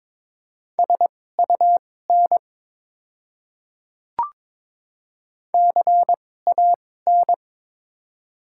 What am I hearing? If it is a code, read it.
Morse code, heard as a keyed tone, about 22 wpm, spaced as for 16 wpm, 700 Hz. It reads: SUN CAN
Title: 2x Character Spacing - Top 500 Words - 22wpm